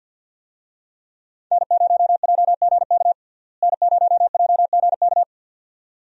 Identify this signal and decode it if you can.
Morse code, heard as a keyed tone, about 52 wpm, 700 Hz. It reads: N0JGK N0JGK
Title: USA Call Signs - 5 Characters - Repeated 2x - 50wpm